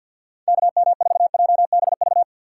Morse code, heard as a keyed tone, about 62 wpm, 700 Hz. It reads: KM4JBV
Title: Speed Racing - USA 2x3 Call Signs 50wpm